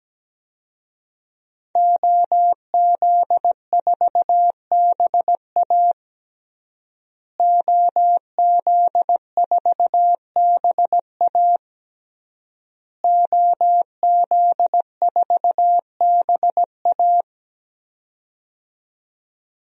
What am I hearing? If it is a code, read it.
Morse code, heard as a keyed tone, about 17 wpm, 700 Hz. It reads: OZ4BA OZ4BA OZ4BA
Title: DXCC-100 Call Signs - 5 Characters - Repeated 3x - 17wpm